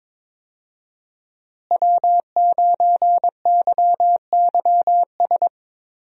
Morse code, heard as a keyed tone, about 22 wpm, 700 Hz. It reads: W9YYS
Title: USA Call Signs - 5 Characters - Repeated 2x - 22wpm